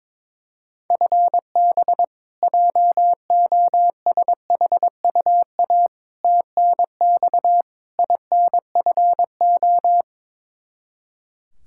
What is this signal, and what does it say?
Morse code, heard as a keyed tone, about 22 wpm, 700 Hz. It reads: FB JOSHUA TNX INFO
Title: K1USN SST QSO Practice 22wpm